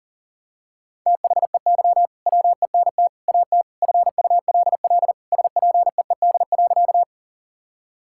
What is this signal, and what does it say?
Morse code, heard as a keyed tone, about 40 wpm, 700 Hz. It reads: THEY WENT AT FULL SPEED.
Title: Sentences from Top 1000 Words - Review Part 5 - 40wpm